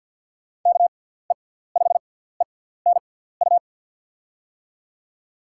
Morse code, heard as a keyed tone, about 48 wpm, spaced as for 17 wpm, 700 Hz. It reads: KE5ENU
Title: USA Call Signs - 6 Characters - 6x Character Spacing - 50wpm